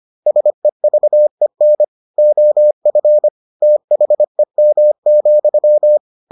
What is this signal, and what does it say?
Morse code, heard as a keyed tone, about 25 wpm, 600 Hz. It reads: SEVEN OF THEM,